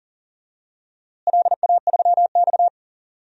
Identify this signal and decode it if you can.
Morse code, heard as a keyed tone, about 40 wpm, 700 Hz. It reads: LA3X
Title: DXCC-100 Call Signs - 4 Characters - Repeated 2x - 40wpm